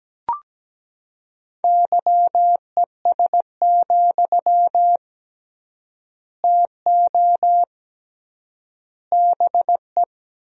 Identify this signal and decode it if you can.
Morse code, heard as a keyed tone, about 17 wpm, 700 Hz. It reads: YES, TO BE
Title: 3x Word-Spacing - Sentences from Top 500 Words - 17wpm